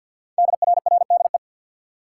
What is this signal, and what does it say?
Morse code, heard as a keyed tone, about 48 wpm, 700 Hz. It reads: DRRDE
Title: Code Groups - 5 Letters - 50wpm